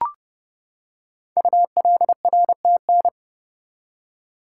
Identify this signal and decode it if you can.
Morse code, heard as a keyed tone, about 30 wpm, 690 Hz. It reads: ULRTN
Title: Code Groups - 5 Letters - 30wpm